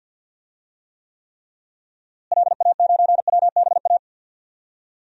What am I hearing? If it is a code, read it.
Morse code, heard as a keyed tone, about 49 wpm, 700 Hz. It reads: LA9WBA
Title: DXCC-100 Call Signs - 6 Characters - Repeated 3x - 50wpm